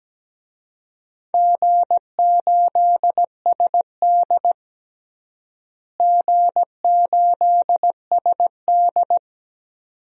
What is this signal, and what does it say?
Morse code, heard as a keyed tone, about 17 wpm, 700 Hz. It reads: G8SD G8SD
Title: DXCC-100 Call Signs - 4 Characters - Repeated 2x - 17wpm